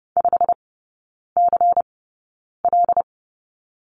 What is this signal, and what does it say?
Morse code, heard as a keyed tone, about 30 wpm, spaced as for 9 wpm, 690 Hz. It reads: 5CL
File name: Lesson 16e - Letter c - Three Characters 30wpm